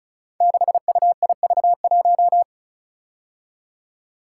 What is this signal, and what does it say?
Morse code, heard as a keyed tone, about 36 wpm, 700 Hz. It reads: 6UIV1
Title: Code Groups - 5 Letter:Numbers - 35wpm